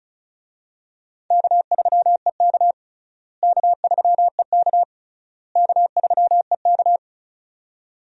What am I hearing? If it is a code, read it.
Morse code, heard as a keyed tone, about 36 wpm, 700 Hz. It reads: K3EK K3EK K3EK
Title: USA Call Signs - 4 Characters - Repeated 3x - 35wpm